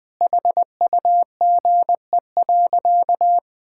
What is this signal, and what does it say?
Morse code, heard as a keyed tone, about 20 wpm, 700 Hz. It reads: HUGE.
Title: Sentences with 5-Word Phrases - 20wpm